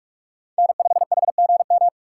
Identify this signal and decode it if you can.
Morse code, heard as a keyed tone, about 46 wpm, 700 Hz. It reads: N5HGM
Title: USA Call Signs - 4x Character Spacing - 45wpm